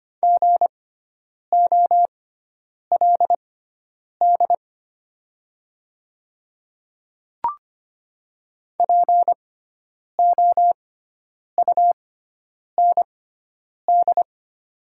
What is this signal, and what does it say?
Morse code, heard as a keyed tone, about 25 wpm, spaced as for 9 wpm, 700 Hz. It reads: GOLD POUND
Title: 6x Character Spacing - Top 500 Words - 25wpm